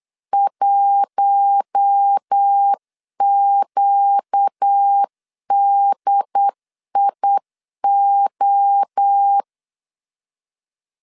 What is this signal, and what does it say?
Morse code, heard as a keyed tone, about 8 wpm, 800 Hz. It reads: 1QDIO
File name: CW / DET / 1 9 1500C S A 8wpm 5637